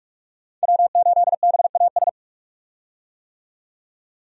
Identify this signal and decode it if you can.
Morse code, heard as a keyed tone, about 46 wpm, 700 Hz. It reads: W8BAS
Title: USA Call Signs - 5 Characters - 45wpm